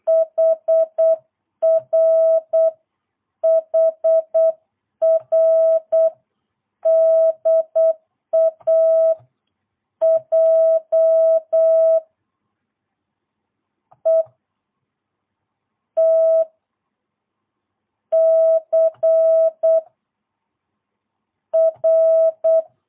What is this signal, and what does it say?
Morse code, heard as a keyed tone, about 8 wpm, spaced as for 5 wpm, 650 Hz. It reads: HR HR DAJ ETCR
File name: jh8cbh / 2022cw